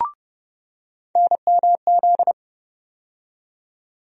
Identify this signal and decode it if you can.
Morse code, heard as a keyed tone, about 30 wpm, 700 Hz. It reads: NMZ
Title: USA Call Sign Postfixes - 3 Letters 30wpm